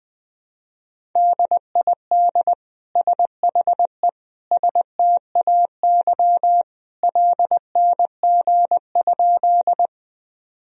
Morse code, heard as a keyed tone, about 20 wpm, 700 Hz. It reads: DID SHE STAY LNG?
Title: Sentences from Top 500 Words and Abbreviations 20wpm